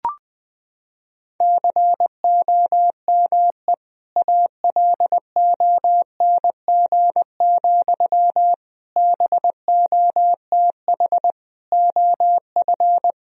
Morse code, heard as a keyed tone, about 20 wpm, 700 Hz. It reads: COME ALONG, BOTH OF
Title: Sentences from Top 1000 Words - Review Part 1 - 20wpm